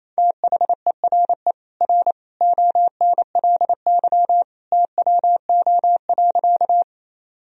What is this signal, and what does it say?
Morse code, heard as a keyed tone, about 28 wpm, 700 Hz. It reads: THERE R ONLY TWO.
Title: Sentences from Top 200 Words and Abbreviations 28wpm